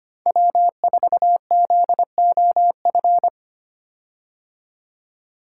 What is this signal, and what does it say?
Morse code, heard as a keyed tone, about 25 wpm, 700 Hz. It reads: W4ZOF
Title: Code Groups - 5 Letter:Numbers - 25wpm